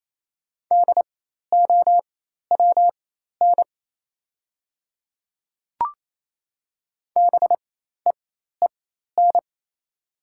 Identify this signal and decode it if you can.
Morse code, heard as a keyed tone, about 28 wpm, spaced as for 13 wpm, 700 Hz. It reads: DOWN BEEN
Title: 4x Character Spacing - Top 100 Words - 28wpm